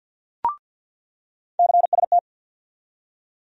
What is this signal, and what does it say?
Morse code, heard as a keyed tone, about 53 wpm, 700 Hz. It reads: CST
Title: USA Call Sign Postfixes - 3 Letters 50wpm